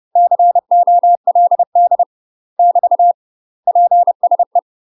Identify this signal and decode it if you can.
Morse code, heard as a keyed tone, about 30 wpm, 700 Hz. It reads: COLD <BT> PSE
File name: UK MASTER QSOs TRAINING / 30WPM / UK MASTER QSOs 0027